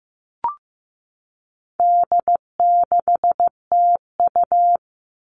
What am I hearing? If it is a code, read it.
Morse code, heard as a keyed tone, about 15 wpm, 700 Hz. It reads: D6TU
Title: All Cut Numbers - 4-Digit 15wpm